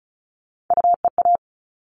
Morse code, heard as a keyed tone, about 34 wpm, 700 Hz. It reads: UEA
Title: All Cut Numbers - 3-Digit 35wpm